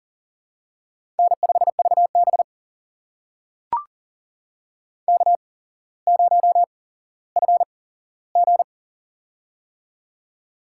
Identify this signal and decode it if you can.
Morse code, heard as a keyed tone, about 40 wpm, spaced as for 17 wpm, 700 Hz. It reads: N5VB K0FG
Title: USA Call Signs - 4 Characters - 8x Character Spacing - 40wpm